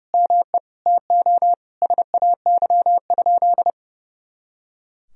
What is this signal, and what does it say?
Morse code, heard as a keyed tone, about 30 wpm, 700 Hz. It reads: ME TO SAY?
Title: Sentences from Top 200 Words and Abbreviations 30wpm